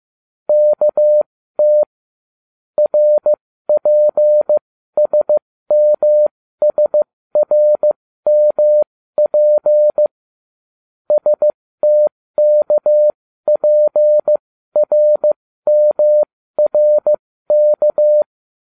Morse code, heard as a keyed tone, about 15 wpm, spaced as for 13 wpm, 600 Hz. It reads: KT RPSMSRMP STKPRMRK